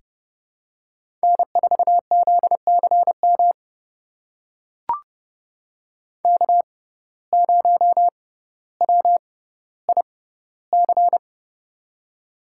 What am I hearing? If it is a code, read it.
Morse code, heard as a keyed tone, about 29 wpm, spaced as for 15 wpm, 700 Hz. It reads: N4ZCM K0WIC
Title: USA Call Signs - 5 Characters - 6x Character Spacing - 30wpm